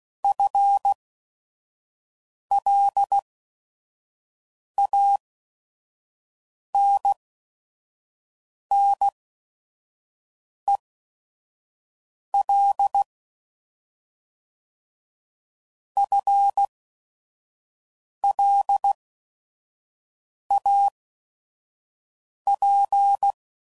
Morse code, heard as a keyed tone, about 16 wpm, spaced as for 5 wpm, 800 Hz. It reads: FLANNEL FLAP
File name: Prefixes / FLs & THs / FL Runs